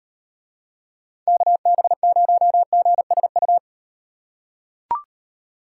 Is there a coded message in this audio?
Yes: Morse code from 1.3 to 3.6 s, heard as a keyed tone, about 38 wpm, 700 Hz. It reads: KB0GSU